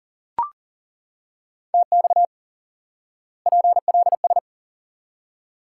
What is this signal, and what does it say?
Morse code, heard as a keyed tone, about 40 wpm, 700 Hz. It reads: TX PLS
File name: IQE Course 3 - Lesson 89 - YAGI - Two Elements - 40wpm